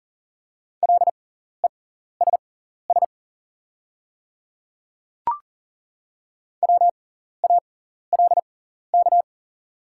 Morse code, heard as a keyed tone, about 41 wpm, spaced as for 14 wpm, 700 Hz. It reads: LESS WALK
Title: 6x Character Spacing - Top 500 Words - 40wpm